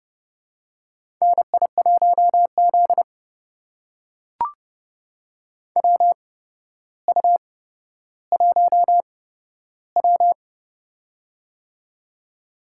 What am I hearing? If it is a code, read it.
Morse code, heard as a keyed tone, about 30 wpm, spaced as for 13 wpm, 700 Hz. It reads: NI1Z WU1W
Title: USA Call Signs - 4 Characters - 8x Character Spacing - 30wpm